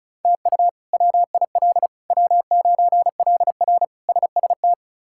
Morse code, heard as a keyed tone, about 36 wpm, 700 Hz. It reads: TU WIL W9LR SST
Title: K1USN SST QSO Practice - 35wpm